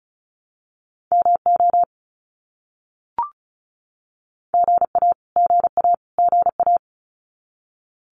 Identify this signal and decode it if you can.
Morse code, heard as a keyed tone, about 34 wpm, 700 Hz. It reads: MO GA GA GA
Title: US States Repeated 3x 35wpm